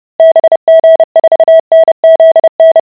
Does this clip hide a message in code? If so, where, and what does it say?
Yes: Morse code from 0.2 to 2.8 s, heard as a keyed tone, about 30 wpm, 650 Hz. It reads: BG4NZN